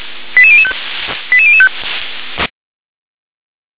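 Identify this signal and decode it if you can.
Five-tone selective call (ZVEI), heard as quick runs of five tones, tones of about 70 ms. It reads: 88005 88005
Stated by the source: Five-tone ZVEI selective call 88040 received on an SDR.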